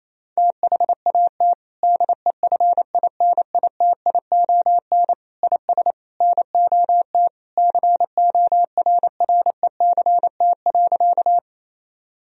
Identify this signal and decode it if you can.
Morse code, heard as a keyed tone, about 28 wpm, 700 Hz. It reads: THAT DEFINITION IS NOT CORRECT.
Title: Sentences from Top 1000 Words - Review Part 4 - 28wpm